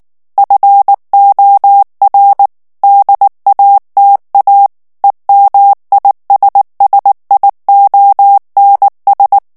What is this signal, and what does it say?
Morse code, heard as a keyed tone, about 19 wpm, 800 Hz. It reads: FOR DATA EMISSIONS
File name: ANGLAIS texte clair 3 vitesse 18 mots minute